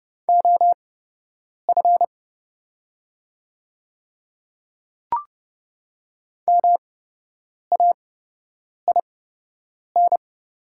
Morse code, heard as a keyed tone, about 31 wpm, spaced as for 8 wpm, 690 Hz. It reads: OF MAIN
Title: 8x Character Spacing - Top 300 Words - 30wpm